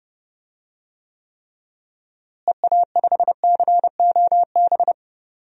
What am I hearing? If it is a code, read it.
Morse code, heard as a keyed tone, about 30 wpm, 700 Hz. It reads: EA5COB
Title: DXCC-100 Call Signs - 6 Characters - Repeated 3x - 30wpm